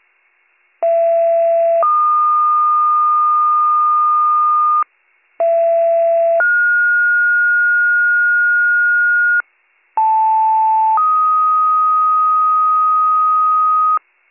Two-tone page, heard as a one-second tone, then a three-second tone, about 670 then 1220 Hz, about 670 then 1430 Hz, about 880 then 1250 Hz. It is heard from 0.8 to 14.0 s.